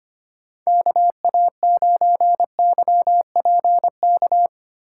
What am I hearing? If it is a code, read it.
Morse code, heard as a keyed tone, about 31 wpm, 700 Hz. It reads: KA9YPK
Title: Speed Racing - USA 2x3 Call Signs 25wpm